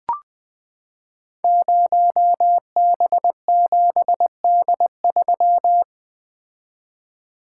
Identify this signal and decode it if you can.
Morse code, heard as a keyed tone, about 20 wpm, 700 Hz. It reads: 0B7D3
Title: Code Groups - 5 Letter:Numbers - 20wpm